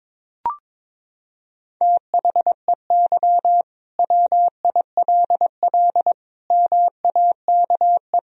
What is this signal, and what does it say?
Morse code, heard as a keyed tone, about 22 wpm, 700 Hz. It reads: THEY WILL MAKE
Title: Sentences from Top 1000 Words - Review Part 3 - 22wpm